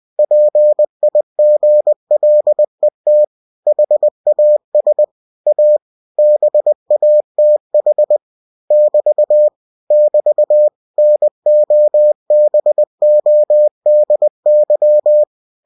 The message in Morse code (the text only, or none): PIGLET HAS A BATH <BT> <BT> NOBODY